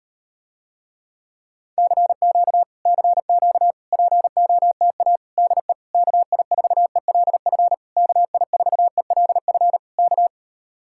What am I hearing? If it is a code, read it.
Morse code, heard as a keyed tone, about 38 wpm, 700 Hz. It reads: CQ CQ POTA DE KI4ELF KI4ELF K